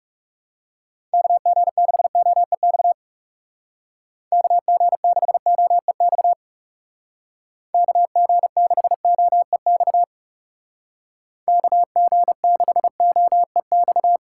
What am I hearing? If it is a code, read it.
Morse code, heard as a keyed tone, about 37 wpm, 700 Hz. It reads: KG6OEX KG6OEX KG6OEX KG6OEX
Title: Speed Racing - USA 2x3 Call Signs 30wpm